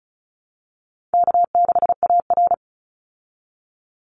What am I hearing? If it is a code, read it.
Morse code, heard as a keyed tone, about 34 wpm, 700 Hz. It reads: K6AR
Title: Easy Call Signs Repeated 2x - 35wpm